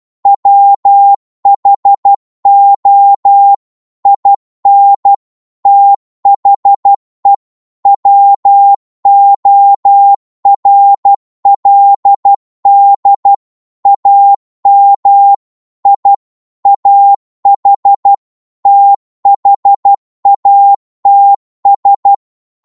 Morse code, heard as a keyed tone, about 12 wpm, 800 Hz. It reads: WHO IN THE WORLD AM I AH THATS